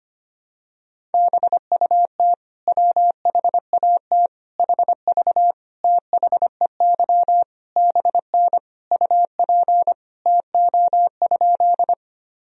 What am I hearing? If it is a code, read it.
Morse code, heard as a keyed tone, about 25 wpm, 700 Hz. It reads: BUT WHAT HV THEY BN UP TO?
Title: Sentences from Top 100 Words and Abbreviations 25wpm